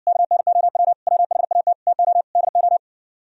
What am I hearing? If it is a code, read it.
Morse code, heard as a keyed tone, about 66 wpm, 700 Hz. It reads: KNOW WHAT TO DO